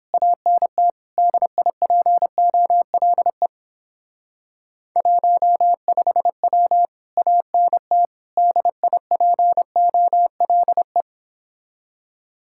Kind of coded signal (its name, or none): Morse code